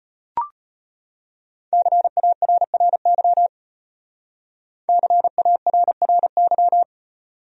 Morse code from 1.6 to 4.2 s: CARRY

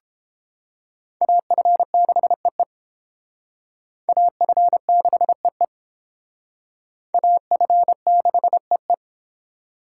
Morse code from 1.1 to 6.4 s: AF6EE AF6EE

